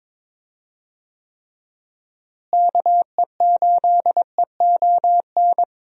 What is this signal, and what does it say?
Morse code, heard as a keyed tone, about 22 wpm, 700 Hz. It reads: KE8EON
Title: USA Call Signs - 6 Characters - Repeated 2x - 22wpm